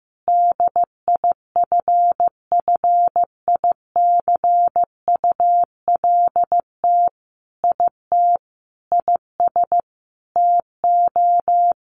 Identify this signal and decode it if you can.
Morse code, heard as a keyed tone, about 14 wpm, 700 Hz. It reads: DIFFICULT IT IS TO
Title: Sentences from Top 1100 Words 15wpm